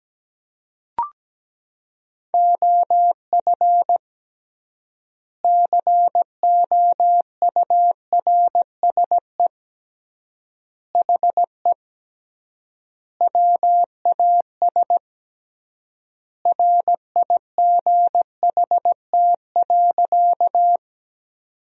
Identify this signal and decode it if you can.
Morse code, heard as a keyed tone, about 17 wpm, 700 Hz. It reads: OF COURSE HE WAS RIGHT.
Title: 3x Word-Spacing - Sentences from Top 500 Words - 17wpm